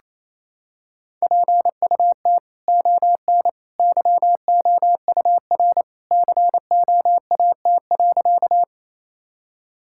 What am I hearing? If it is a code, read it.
Morse code, heard as a keyed tone, about 28 wpm, 700 Hz. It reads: PUT ON YOUR COAT.
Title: Sentences from Top 1600 Words 28wpm